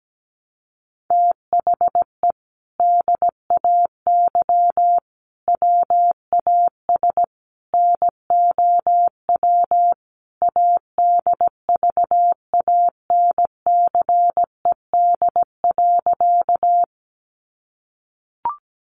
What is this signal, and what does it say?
Morse code, heard as a keyed tone, about 17 wpm, 700 Hz. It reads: THE DAY WAS NOW ADVANCED.
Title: Sentences from Top 1200 Words 17wpm